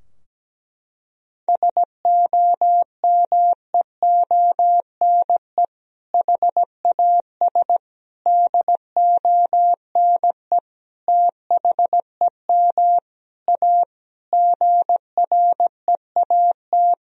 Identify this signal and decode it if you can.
Morse code, heard as a keyed tone, about 17 wpm, 700 Hz. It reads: SOMEONE HAS DONE THEM A GREAT
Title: Sentences from Top 1000 Words - Review Part 4 - 17wpm